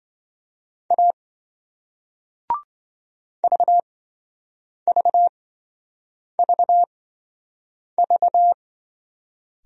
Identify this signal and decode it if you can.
Morse code, heard as a keyed tone, about 26 wpm, spaced as for 7 wpm, 700 Hz. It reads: A VVVV